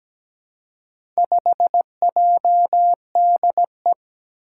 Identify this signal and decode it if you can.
Morse code, heard as a keyed tone, about 17 wpm, 700 Hz. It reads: 5JDE
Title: Four Challenging Letter-Number Combinations 17wpm